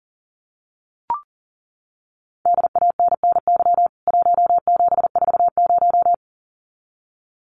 Morse code, heard as a keyed tone, about 40 wpm, 700 Hz. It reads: DANNY 1740
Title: CWT Exchange 40wpm